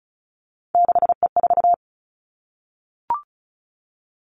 Morse code, heard as a keyed tone, about 36 wpm, 700 Hz. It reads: 6E4
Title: Three Challenging Letter-Number Combinations 35wpm